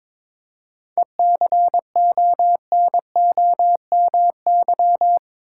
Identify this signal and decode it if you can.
Morse code, heard as a keyed tone, about 22 wpm, 700 Hz. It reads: ECONOMY
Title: Seven Letter Words - 22wpm